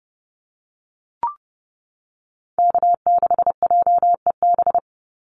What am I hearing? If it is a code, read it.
Morse code, heard as a keyed tone, about 30 wpm, 700 Hz. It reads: K6JEB
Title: Call Signs - 30wpm